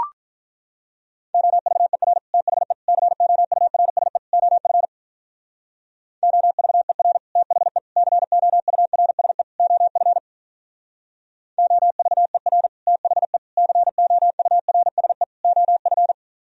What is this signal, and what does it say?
Morse code, heard as a keyed tone, about 44 wpm, 700 Hz. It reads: OVER THE COURSE OF OVER THE COURSE OF OVER THE COURSE OF
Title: Speed-Racing - Sets of 4 Words 35wpm